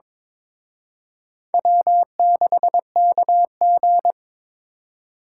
Morse code, heard as a keyed tone, about 26 wpm, 700 Hz. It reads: W6KG
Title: Speed Racing - USA 1x2 Call Signs 22wpm